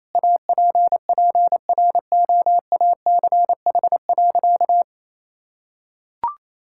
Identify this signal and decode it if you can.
Morse code, heard as a keyed tone, about 28 wpm, 700 Hz. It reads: APPROACH.
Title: Sentences from Top 1000 Words - Review Part 6 - 28wpm